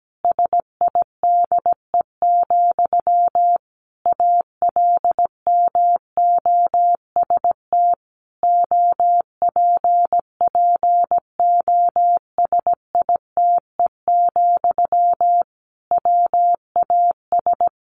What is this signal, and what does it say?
Morse code, heard as a keyed tone, about 17 wpm, 700 Hz. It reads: SIDE, ALMOST OPPOSITE, WAS
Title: Sentences from Top 1300 Words 17wpm